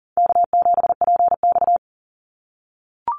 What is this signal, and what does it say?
Morse code, heard as a keyed tone, about 40 wpm, 700 Hz. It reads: K7PX
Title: NA SOTA Activator Callsigns 40wpm